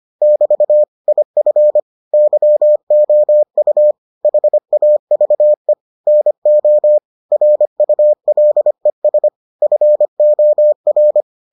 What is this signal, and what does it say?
Morse code, heard as a keyed tone, about 25 wpm, 600 Hz. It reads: <BT> IF YOU HAVE NO RULES FOR